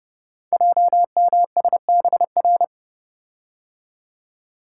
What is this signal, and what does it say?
Morse code, heard as a keyed tone, about 30 wpm, 700 Hz. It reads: JMSBR
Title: Code Groups - 5 Letters - 30wpm